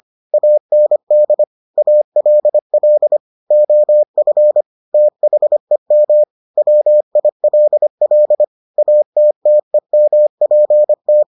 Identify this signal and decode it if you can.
Morse code, heard as a keyed tone, about 25 wpm, 600 Hz. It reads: AND ALL OF THEM WILL ATTEMPT